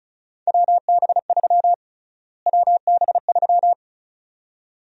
Morse code, heard as a keyed tone, about 36 wpm, 700 Hz. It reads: WB3 WB3
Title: USA Call Sign Prefixes - Repeated 2x - 35wpm